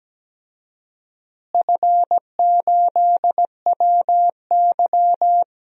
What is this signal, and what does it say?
Morse code, heard as a keyed tone, about 17 wpm, 700 Hz. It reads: F8WY
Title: DXCC-100 Call Signs - 4 Characters - Repeated 3x - 17wpm